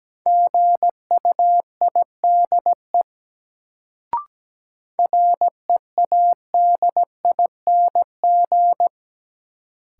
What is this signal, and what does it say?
Morse code, heard as a keyed tone, about 17 wpm, 700 Hz. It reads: GUIDE READING
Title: Sentences from Top 600 Words - New Word Drill - 17wpm